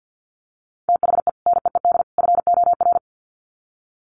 Morse code, heard as a keyed tone, about 50 wpm, 700 Hz. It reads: THE NEED FOR
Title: Sets of 3 Words 50wpm